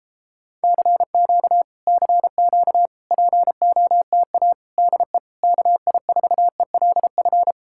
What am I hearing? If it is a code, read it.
Morse code, heard as a keyed tone, about 38 wpm, 700 Hz. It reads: CQ CQ POTA DE KI4ELF